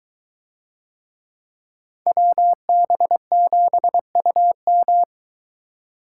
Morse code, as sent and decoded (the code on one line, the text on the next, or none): .-- -... --... ..- --
WB7UM